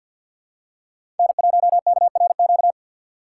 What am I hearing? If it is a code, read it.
Morse code, heard as a keyed tone, about 46 wpm, 700 Hz. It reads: N1KRQ
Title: USA Call Signs - 2x Character Spacing - 50wpm